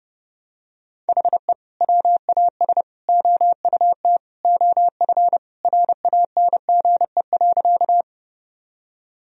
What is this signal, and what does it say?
Morse code, heard as a keyed tone, about 32 wpm, 700 Hz. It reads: HE WAS OUT OF RANGE.
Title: Sentences from Top 1000 Words - Review Part 4 - 30wpm